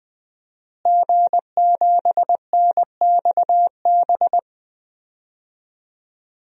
Morse code, heard as a keyed tone, about 20 wpm, 700 Hz. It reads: G7NXB
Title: DXCC-100 Call Signs - 5 Characters - Repeated 2x - 20wpm